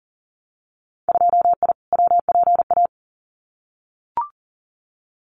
Morse code, heard as a keyed tone, about 42 wpm, 700 Hz. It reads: 2I WPA
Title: Winter Field Day Exchanges 40wpm